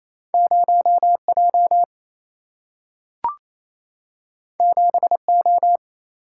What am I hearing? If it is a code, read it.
Morse code, heard as a keyed tone, about 28 wpm, 700 Hz. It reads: 0J 7O
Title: Single-Letter Single-Number 28wpm